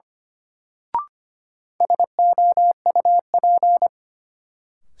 Morse code, heard as a keyed tone, about 25 wpm, 700 Hz. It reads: SOUP
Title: Sentences from Top 1000 Words - Review Part 6 - New Word Drill - 25wpm